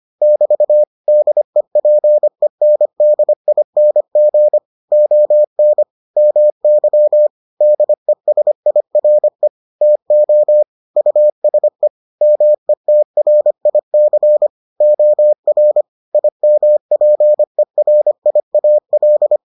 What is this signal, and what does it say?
Morse code, heard as a keyed tone, about 25 wpm, 600 Hz. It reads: <BT> DEPENDING ON MY DESIRE TO USE METRIC OR IMPERIAL